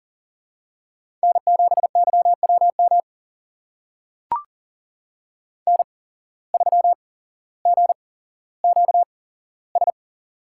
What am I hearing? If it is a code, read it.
Morse code, heard as a keyed tone, about 39 wpm, spaced as for 17 wpm, 700 Hz. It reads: N7YWM N3GQS
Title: USA Call Signs - 8x Character Spacing - 40wpm